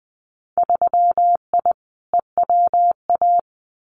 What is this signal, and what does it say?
Morse code, heard as a keyed tone, about 20 wpm, 700 Hz. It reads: 3I EWA
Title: Winter Field Day Exchanges Encore 20wpm